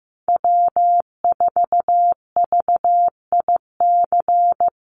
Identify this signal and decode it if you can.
Morse code, heard as a keyed tone, about 15 wpm, 700 Hz. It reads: W4VIC
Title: Call Signs Repeated 3x - 15wpm